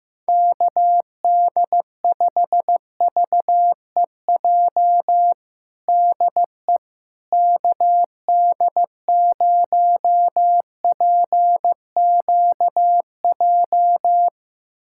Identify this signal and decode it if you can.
Morse code, heard as a keyed tone, about 15 wpm, 700 Hz. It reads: KD5VEJ DE KD0PQJ
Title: USA Callsign DE Callsign - 6 Characters - 15wpm